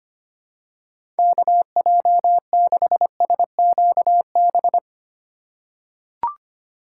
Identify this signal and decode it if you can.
Morse code, heard as a keyed tone, about 25 wpm, 700 Hz. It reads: KJ6SQB
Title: USA Call Signs - 6 Characters - 4x Character Spacing - 25wpm